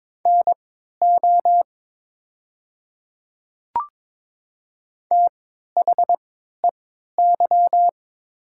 Morse code, heard as a keyed tone, about 22 wpm, spaced as for 12 wpm, 700 Hz. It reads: NO THEY
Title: 3x Character Spacing - Top 200 Words - 22wpm